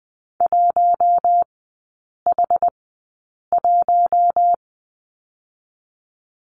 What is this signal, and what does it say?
Morse code, heard as a keyed tone, about 19 wpm, spaced as for 8 wpm, 700 Hz. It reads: 1H1
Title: 067 Lesson 12e - Letter d - Three Characters 20wpm